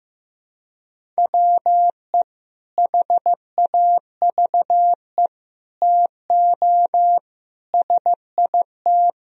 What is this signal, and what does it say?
Morse code, heard as a keyed tone, about 15 wpm, 700 Hz. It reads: WE HAVE TO SIT
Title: Sentences from Top 1000 Words - Review Part 1 - 15wpm